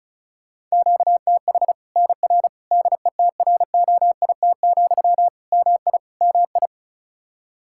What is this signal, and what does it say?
Morse code, heard as a keyed tone, about 34 wpm, 700 Hz. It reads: QTH NR DETROIT, MI MI
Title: Practice QSO - Exchange 1 - Beginning - 35wpm